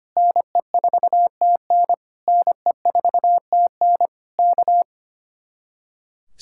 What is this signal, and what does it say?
Morse code, heard as a keyed tone, about 24 wpm, 700 Hz. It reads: NE4TN NE4TN K
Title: Speed-Racing Common QSO Phrases - 25wpm